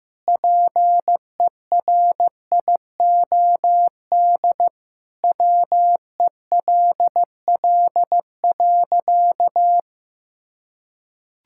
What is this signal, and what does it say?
Morse code, heard as a keyed tone, about 16 wpm, 700 Hz. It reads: PERIOD WELL.
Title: Sentences from Top 1000 Words - Review Part 5 - 15wpm